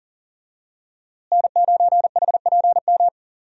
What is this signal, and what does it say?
Morse code, heard as a keyed tone, about 37 wpm, 700 Hz. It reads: N9HPM
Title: USA Call Signs - 5 Characters - Repeated 3x - 40wpm